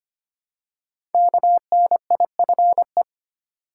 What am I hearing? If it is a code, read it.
Morse code, heard as a keyed tone, about 25 wpm, 700 Hz. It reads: KNIFE
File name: 5 Letter Words - First 3 Letters Given - 25wpm